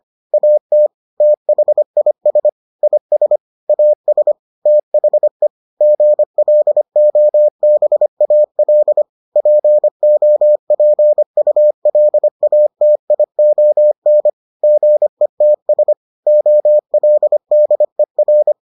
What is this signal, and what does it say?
Morse code, heard as a keyed tone, about 25 wpm, 600 Hz. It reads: AT THIS IS AS THE GLOBAL POPULATION GETS OLDER